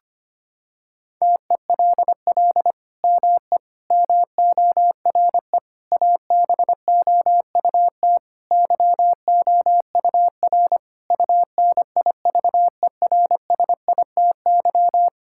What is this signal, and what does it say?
Morse code, heard as a keyed tone, about 25 wpm, 700 Hz. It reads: TELL ME MORE ABOUT YOUR UNIVERSITY
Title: Sentences from Top 1000 Words - Review Part 5 - 25wpm